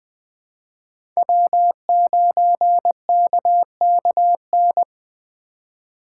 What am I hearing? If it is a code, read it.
Morse code, heard as a keyed tone, about 21 wpm, 700 Hz. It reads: W9KKN
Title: USA Call Signs - 5 Characters - 8x Character Spacing - 20wpm